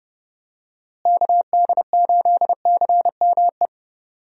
Morse code, heard as a keyed tone, about 30 wpm, 700 Hz. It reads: KD8CME
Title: USA Call Signs - 6 Characters - Repeated 3x - 30wpm